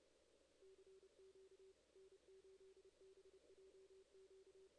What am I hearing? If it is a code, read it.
Morse code, heard as a keyed tone, about 29 wpm, 380 Hz, with noise. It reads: CQ N8BJQ
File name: 2016 JAN CW NAQP / 2016 Jan naqp k5ot - 0-9